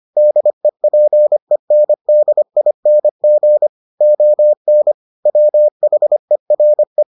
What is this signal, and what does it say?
Morse code, heard as a keyed tone, about 25 wpm, 600 Hz. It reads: DEPENDING ON WHERE